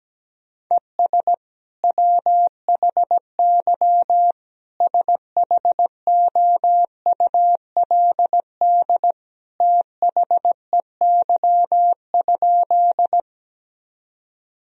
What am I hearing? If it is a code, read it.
Morse code, heard as a keyed tone, about 17 wpm, 700 Hz. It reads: ES WHY SHOULD THEY?
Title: Sentences from Top 300 Words and Abbreviations 17wpm